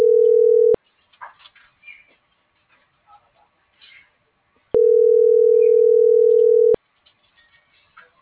Ringback tone (North America), 440 + 480 Hz.